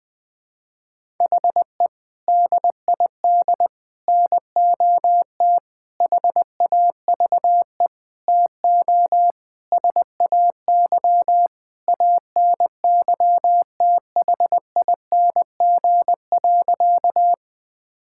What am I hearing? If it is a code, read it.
Morse code, heard as a keyed tone, about 20 wpm, 700 Hz. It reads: HE DID NOT HAVE TO SAY ANYTHING.
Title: Sentences with 5-Word Phrases - Spoken Beginning - 20wpm